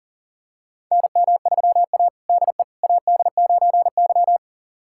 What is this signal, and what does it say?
Morse code, heard as a keyed tone, about 40 wpm, 700 Hz. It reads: NM3A DE AD9Y
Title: USA Callsign DE Callsign - 4 Characters - 40wpm